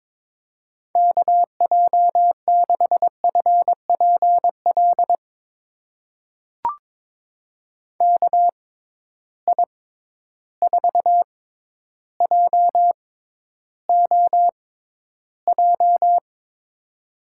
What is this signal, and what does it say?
Morse code, heard as a keyed tone, about 22 wpm, spaced as for 11 wpm, 700 Hz. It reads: KJ6FPL KI4JOJ
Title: USA Call Signs - 6 Characters - 6x Character Spacing - 22wpm